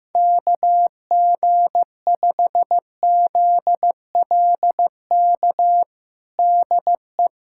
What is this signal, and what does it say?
Morse code, heard as a keyed tone, about 15 wpm, 700 Hz. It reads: KG5ZLK DE
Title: USA Callsign DE Callsign - 6 Characters - 15wpm